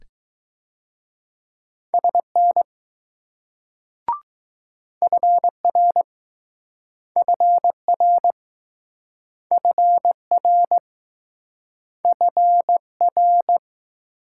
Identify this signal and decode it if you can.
Morse code, heard as a keyed tone, about 20 wpm, 700 Hz. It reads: SN FR FR FR FR